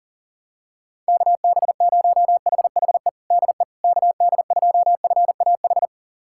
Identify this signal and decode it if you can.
Morse code, heard as a keyed tone, about 40 wpm, 700 Hz. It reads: KB0HHE DE KD2FAH
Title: USA Callsign DE Callsign 40wpm